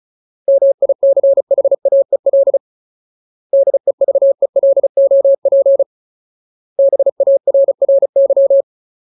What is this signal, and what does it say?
Morse code, heard as a keyed tone, about 34 wpm, 550 Hz. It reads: MICHAEL DEVELOP BARRY
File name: Ultimate Word List / 35WPM Ultimate